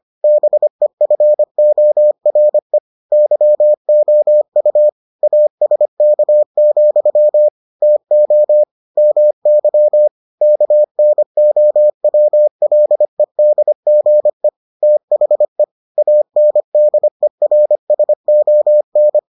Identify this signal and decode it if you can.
Morse code, heard as a keyed tone, about 25 wpm, 600 Hz. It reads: BEFORE YOU ASK, TO MY KNOWLEDGE THE ANDERSON